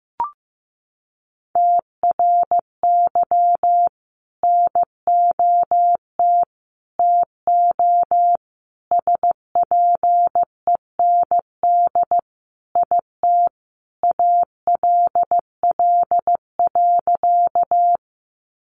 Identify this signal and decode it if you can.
Morse code, heard as a keyed tone, about 15 wpm, 700 Hz. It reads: TRY NOT TO SPEND IT ALL.
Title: Sentences from Top 800 Words 15wpm